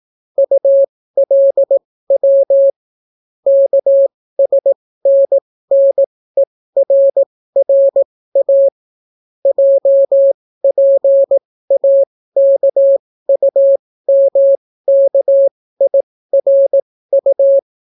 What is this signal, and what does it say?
Morse code, heard as a keyed tone, about 18 wpm, spaced as for 14 wpm, 550 Hz. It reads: ULW KSNNERRA JPAKUMKIRU